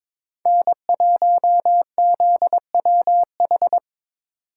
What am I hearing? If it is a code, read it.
Morse code, heard as a keyed tone, about 26 wpm, 690 Hz. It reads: N1ZWH
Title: Speed Racing - USA Call Signs 22wpm